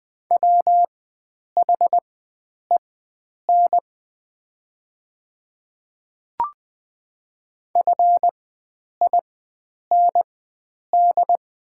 Morse code, heard as a keyed tone, about 20 wpm, spaced as for 9 wpm, 700 Hz. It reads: WHEN FIND
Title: 4x Character Spacing - Top 100 Words - 20wpm